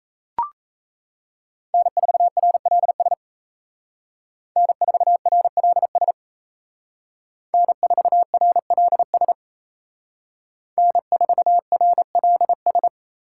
Morse code, heard as a keyed tone, about 35 wpm, 700 Hz. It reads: N4RLS N4RLS N4RLS N4RLS